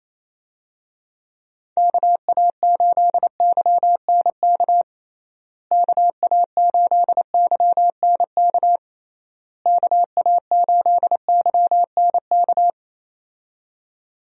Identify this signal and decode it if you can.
Morse code, heard as a keyed tone, about 27 wpm, 700 Hz. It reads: KA8YNK KA8YNK KA8YNK
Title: USA Call Signs - 6 Characters - Repeated 3x - 28wpm